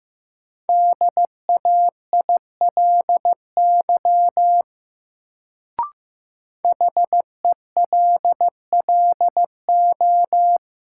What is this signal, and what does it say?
Morse code, heard as a keyed tone, about 15 wpm, 700 Hz. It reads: DAILY HELLO
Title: Sentences from Top 1100 Words - New Word Drill - 15wpm